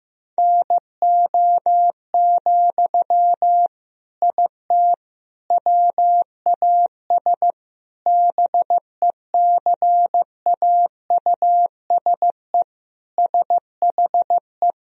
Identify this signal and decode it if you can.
Morse code, heard as a keyed tone, about 15 wpm, 700 Hz. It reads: NO, IT WAS BECAUSE SHE
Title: Sentences from Top 1000 Words - Review Part 4 - 15wpm